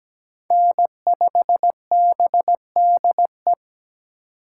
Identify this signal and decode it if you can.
Morse code, heard as a keyed tone, about 20 wpm, 700 Hz. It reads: N5BDE
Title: Speed Racing - USA 1x3 Call Signs 17wpm